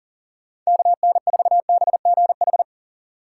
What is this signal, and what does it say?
Morse code, heard as a keyed tone, about 37 wpm, 700 Hz. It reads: KN4BGH
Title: USA Call Signs - 6 Characters - 6x Character Spacing - 40wpm